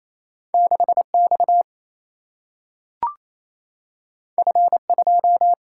Morse code, heard as a keyed tone, about 28 wpm, 700 Hz. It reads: 6 X F 2
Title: Single-Letter Single-Number 28wpm